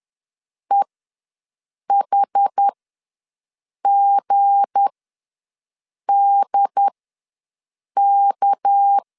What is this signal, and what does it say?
Morse code, heard as a keyed tone, about 10 wpm, spaced as for 6 wpm, 800 Hz. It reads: EHGDK
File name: LEZIONI / 7 GF9Z / 7 GF ws 10wpm 3918